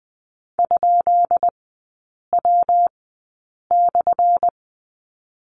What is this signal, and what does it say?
Morse code, heard as a keyed tone, about 20 wpm, spaced as for 8 wpm, 700 Hz. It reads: ?W/
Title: 205 Lesson 33e - Letter j - Three Characters 20wpm